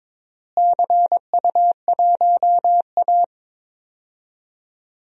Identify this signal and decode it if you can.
Morse code, heard as a keyed tone, about 22 wpm, 700 Hz. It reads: CU1A
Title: DXCC-100 Call Signs - 4 Characters - Repeated 3x - 22wpm